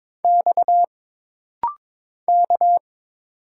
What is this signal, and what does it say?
Morse code, heard as a keyed tone, about 22 wpm, 700 Hz. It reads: X K
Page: This rapid-fire timing is geared to push your ICR (Instant Character Recognition) skills.